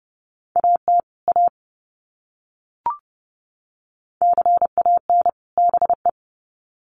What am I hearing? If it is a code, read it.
Morse code, heard as a keyed tone, about 30 wpm, 700 Hz. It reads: AT A CAN BE
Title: Sets of 2 Words 30wpm